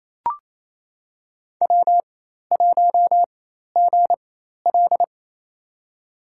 Morse code, heard as a keyed tone, about 28 wpm, spaced as for 13 wpm, 700 Hz. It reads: W1GL
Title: USA Call Signs - 4 Characters - 4x Character Spacing - 28wpm